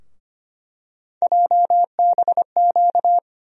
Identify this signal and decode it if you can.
Morse code, heard as a keyed tone, about 25 wpm, 700 Hz. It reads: JBQ
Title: USA Call Sign Postfixes - 3 Letters 25wpm